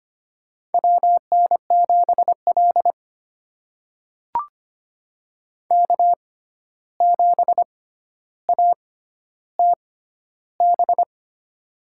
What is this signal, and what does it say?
Morse code, heard as a keyed tone, about 25 wpm, spaced as for 12 wpm, 700 Hz. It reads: WN7L K7ATB
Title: USA Call Signs - 6x Character Spacing - 25wpm